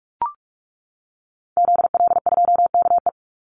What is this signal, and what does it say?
Morse code, heard as a keyed tone, about 44 wpm, 700 Hz. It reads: ZL2KE
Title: Call Signs - 45wpm